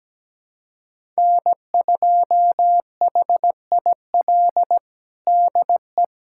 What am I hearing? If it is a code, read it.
Morse code, heard as a keyed tone, about 17 wpm, 700 Hz. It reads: N2HIL DE
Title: USA Callsign DE Callsign - 5 Characters - 17wpm